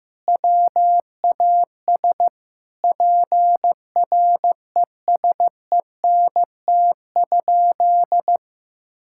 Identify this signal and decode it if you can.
Morse code, heard as a keyed tone, about 16 wpm, 700 Hz. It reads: WAS PRESENT?
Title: Sentences from Top 1000 Words - Review Part 5 - 15wpm